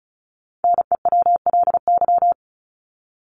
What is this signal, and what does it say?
Morse code, heard as a keyed tone, about 34 wpm, 700 Hz. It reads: NEWLY
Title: LY Suffix - Top 100 Words with Base Words 35wpm